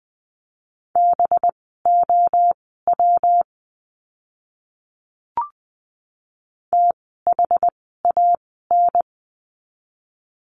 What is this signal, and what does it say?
Morse code, heard as a keyed tone, about 20 wpm, spaced as for 15 wpm, 700 Hz. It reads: BOW THAN
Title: 185 Lesson 30f - Letter b - Words 20wpm